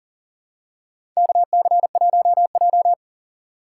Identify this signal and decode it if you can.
Morse code, heard as a keyed tone, about 37 wpm, 700 Hz. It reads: KC1J
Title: USA Call Signs - 4 Characters - 4x Character Spacing - 40wpm